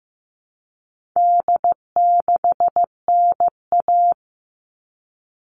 Morse code, heard as a keyed tone, about 15 wpm, 700 Hz. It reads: D6NA
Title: All Cut Numbers - 4-Digit 15wpm